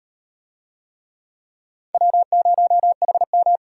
Morse code, heard as a keyed tone, about 38 wpm, 700 Hz. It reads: W0HM